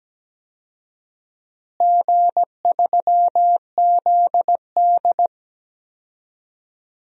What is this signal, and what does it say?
Morse code, heard as a keyed tone, about 17 wpm, 700 Hz. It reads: G3ZD
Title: DXCC-100 Call Signs - 4 Characters - Repeated 3x - 17wpm